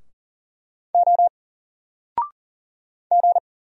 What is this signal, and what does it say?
Morse code, heard as a keyed tone, about 39 wpm, 700 Hz. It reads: O G
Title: Single Letter-Number 40wpm